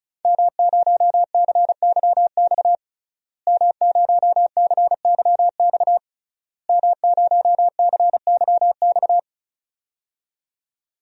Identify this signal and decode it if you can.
Morse code, heard as a keyed tone, about 36 wpm, 700 Hz. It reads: M0CYX M0CYX M0CYX
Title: DXCC-100 Call Signs - 5 Characters - Repeated 3x - 35wpm